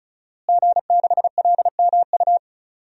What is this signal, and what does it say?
Morse code, heard as a keyed tone, about 35 wpm, 700 Hz. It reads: G6LMU
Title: DXCC-100 Call Signs - 5 Characters - Repeated 3x - 35wpm